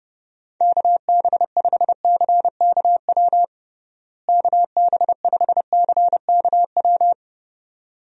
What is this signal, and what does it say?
Morse code, heard as a keyed tone, about 30 wpm, 700 Hz. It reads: KB5CKW KB5CKW
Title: USA Call Signs - 6 Characters - Repeated 2x - 30wpm